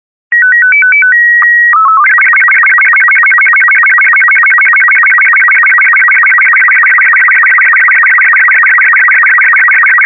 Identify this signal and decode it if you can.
SSTV (slow-scan television), Robot 36.